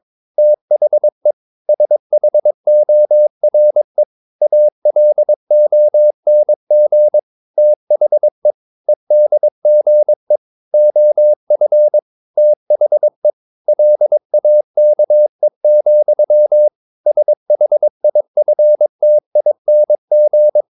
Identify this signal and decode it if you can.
Morse code, heard as a keyed tone, about 22 wpm, 600 Hz. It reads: THE SHORE ALONG THE EDGE OF THE LAKE, SHIFTING
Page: Romanen Never af Ken Follet på engelsk.